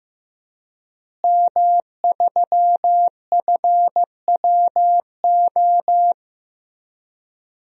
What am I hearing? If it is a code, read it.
Morse code, heard as a keyed tone, about 16 wpm, 700 Hz. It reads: M3FWO
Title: DXCC-100 Call Signs - 5 Characters - Repeated 3x - 15wpm